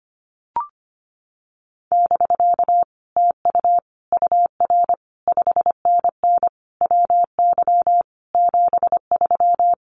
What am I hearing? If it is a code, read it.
Morse code, heard as a keyed tone, about 25 wpm, 700 Hz. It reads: <BK> TU UR 5NN WY 73
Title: Common QSO Phrases Encore 25wpm